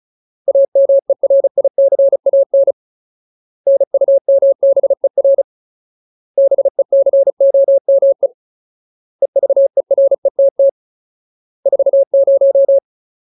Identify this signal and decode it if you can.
Morse code, heard as a keyed tone, about 34 wpm, 550 Hz. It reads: AMERICAN NUMBER BECOME EVERETT 40
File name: Ultimate Word List / 35WPM Ultimate